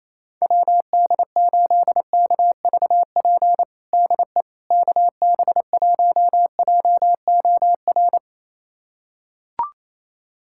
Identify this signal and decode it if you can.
Morse code, heard as a keyed tone, about 28 wpm, 700 Hz. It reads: WD8KVP DE KB1JOR
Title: USA Callsign DE Callsign - 6 Characters - 28wpm